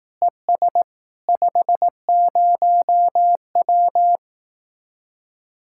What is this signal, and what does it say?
Morse code, heard as a keyed tone, about 18 wpm, 700 Hz. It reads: ES 50W
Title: Speed-Racing Common QSO Phrases Encore - 15wpm